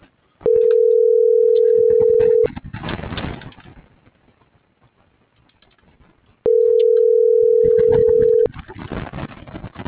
Ringback tone (North America), 440 + 480 Hz.
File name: CENTRAL MABE / out luis.palacios